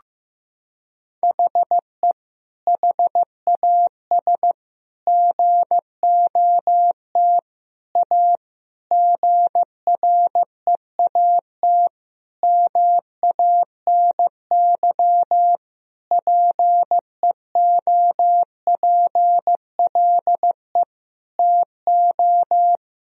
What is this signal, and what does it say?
Morse code, heard as a keyed tone, about 16 wpm, 700 Hz. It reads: HE HAS GOT A GREAT MANY PEOPLE TO
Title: Sentences from Top 1000 Words - Review Part 1 - 15wpm